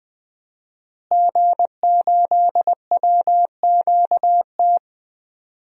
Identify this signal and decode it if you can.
Morse code, heard as a keyed tone, about 20 wpm, 700 Hz. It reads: G8WQT
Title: DXCC-100 Call Signs - 5 Characters - Repeated 2x - 20wpm